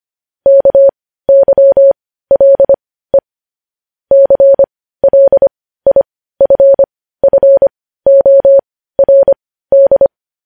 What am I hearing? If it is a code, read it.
Morse code, heard as a keyed tone, about 25 wpm, spaced as for 15 wpm, 550 Hz. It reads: KYLE CLIFFORD
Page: News Headlines in Morse Code at 15 WPM